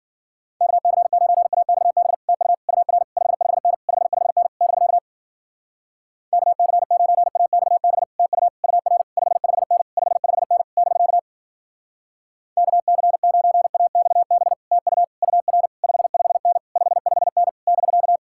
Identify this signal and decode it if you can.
Morse code, heard as a keyed tone, about 51 wpm, 700 Hz. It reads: KC9AXB TU UR 55N 55N <BK> KC9AXB TU UR 55N 55N <BK> KC9AXB TU UR 55N 55N <BK>